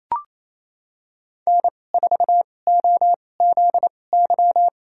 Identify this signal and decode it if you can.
Morse code, heard as a keyed tone, about 28 wpm, spaced as for 20 wpm, 700 Hz. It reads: N4OZY
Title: USA Call Signs - 5 Characters - 2x Character Spacing - 28wpm